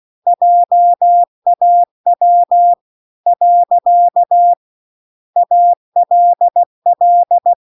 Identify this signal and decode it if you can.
Morse code, heard as a keyed tone, about 16 wpm, 700 Hz. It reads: JAW . ALL
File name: CW-The-Big-Twoheaded-River-2-16WPM